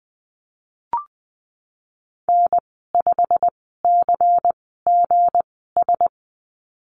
Morse code, heard as a keyed tone, about 20 wpm, spaced as for 15 wpm, 700 Hz. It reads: N5CGS
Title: 173 Lesson 28g - Number 9 - Call Signs 20wpm